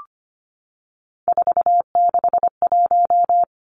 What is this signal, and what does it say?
Morse code, heard as a keyed tone, about 25 wpm, 700 Hz. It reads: 461
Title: Three Challenging Letter-Number Combinations 25wpm